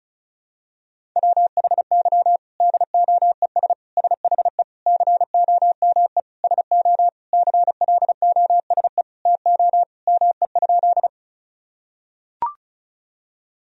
Morse code, heard as a keyed tone, about 36 wpm, 700 Hz. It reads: WHY DOES SHE COME SO CLOSE TO ME?
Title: Sentences from Top 1000 Words - Review Part 1 - 35wpm